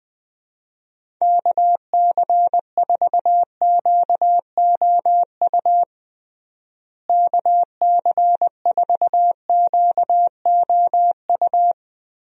Morse code, heard as a keyed tone, about 20 wpm, 700 Hz. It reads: KC4QOU KC4QOU
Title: USA Call Signs - 6 Characters - Repeated 2x - 20wpm